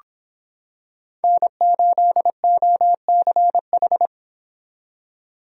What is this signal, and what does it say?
Morse code, heard as a keyed tone, about 26 wpm, 700 Hz. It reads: N8OCH